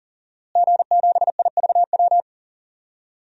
Morse code, heard as a keyed tone, about 38 wpm, 700 Hz. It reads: G7IVW
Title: DXCC-100 Call Signs - 5 Characters - Repeated 3x - 40wpm